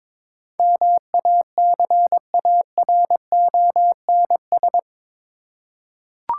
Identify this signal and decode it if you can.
Morse code, heard as a keyed tone, about 22 wpm, 700 Hz. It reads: MACARONS
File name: Difficult Words with Hints - 22wpm